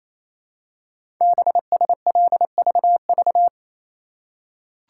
Morse code, heard as a keyed tone, about 28 wpm, 700 Hz. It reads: BSLVV
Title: Code Groups - 5 Letters - 28wpm